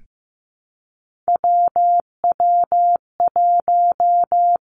Morse code, heard as a keyed tone, about 15 wpm, 700 Hz. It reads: WW1
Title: Three Challenging Letter-Number Combinations 15wpm